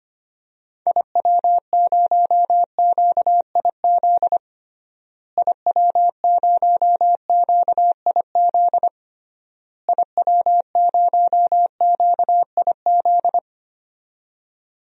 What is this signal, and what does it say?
Morse code, heard as a keyed tone, about 25 wpm, 700 Hz. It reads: IW0QIZ IW0QIZ IW0QIZ
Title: DXCC-100 Call Signs - 6 Characters - Repeated 3x - 25wpm